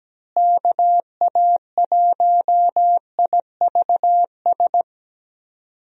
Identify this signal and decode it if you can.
Morse code, heard as a keyed tone, about 17 wpm, 700 Hz. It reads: KA1IVS
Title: USA Call Signs - 6 Characters - Repeated 2x - 17wpm